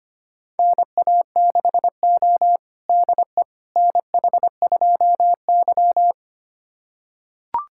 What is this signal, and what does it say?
Morse code, heard as a keyed tone, about 25 wpm, 700 Hz. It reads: NA6O DE NH2Y
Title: USA Callsign DE Callsign - 4 Characters - 25wpm